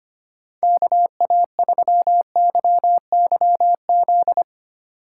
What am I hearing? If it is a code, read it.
Morse code, heard as a keyed tone, about 25 wpm, 700 Hz. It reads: KA3YYZ
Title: USA Call Signs - 6 Characters - 3x Character Spacing - 25wpm